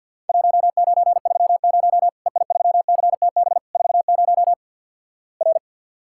Morse code, heard as a keyed tone, about 50 wpm, 700 Hz. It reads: 1930 EI3CTB 40 R
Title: Worked All Europe DX Contest - QTC Exchange with Repeats - 50wpm